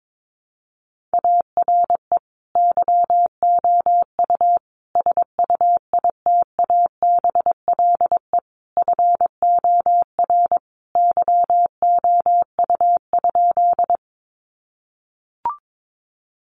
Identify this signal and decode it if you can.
Morse code, heard as a keyed tone, about 22 wpm, 700 Hz. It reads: ARE YOU SUITABLE FOR YOU?
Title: Sentences from Top 1400 Words 22wpm